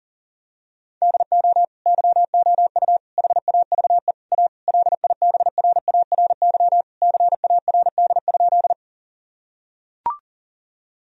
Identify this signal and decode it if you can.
Morse code, heard as a keyed tone, about 40 wpm, 700 Hz. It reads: DO YOU HAVE A LIBRARY CARD?
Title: Sentences from Top 1000 Words - Review Part 4 - 40wpm